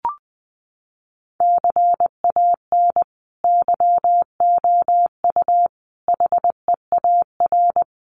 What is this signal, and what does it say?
Morse code, heard as a keyed tone, about 20 wpm, 700 Hz. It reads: CAN YOU HEAR
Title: Sentences from Top 300 Words 20wpm